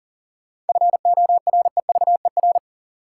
Morse code, heard as a keyed tone, about 40 wpm, 700 Hz. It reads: FOREVER
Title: Sentences from Top 1600 Words - New Word Drill - 40wpm